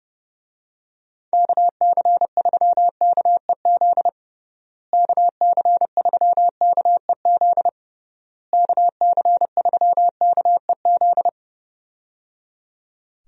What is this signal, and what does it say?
Morse code, heard as a keyed tone, about 29 wpm, 700 Hz. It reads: KC3KEZ KC3KEZ KC3KEZ
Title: USA Call Signs - 6 Characters - Repeated 3x - 30wpm